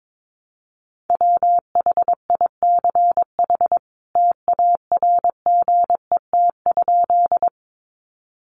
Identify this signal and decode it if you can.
Morse code, heard as a keyed tone, about 22 wpm, 700 Hz. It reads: WHICH TARGET?
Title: Sentences from Top 1200 Words 22wpm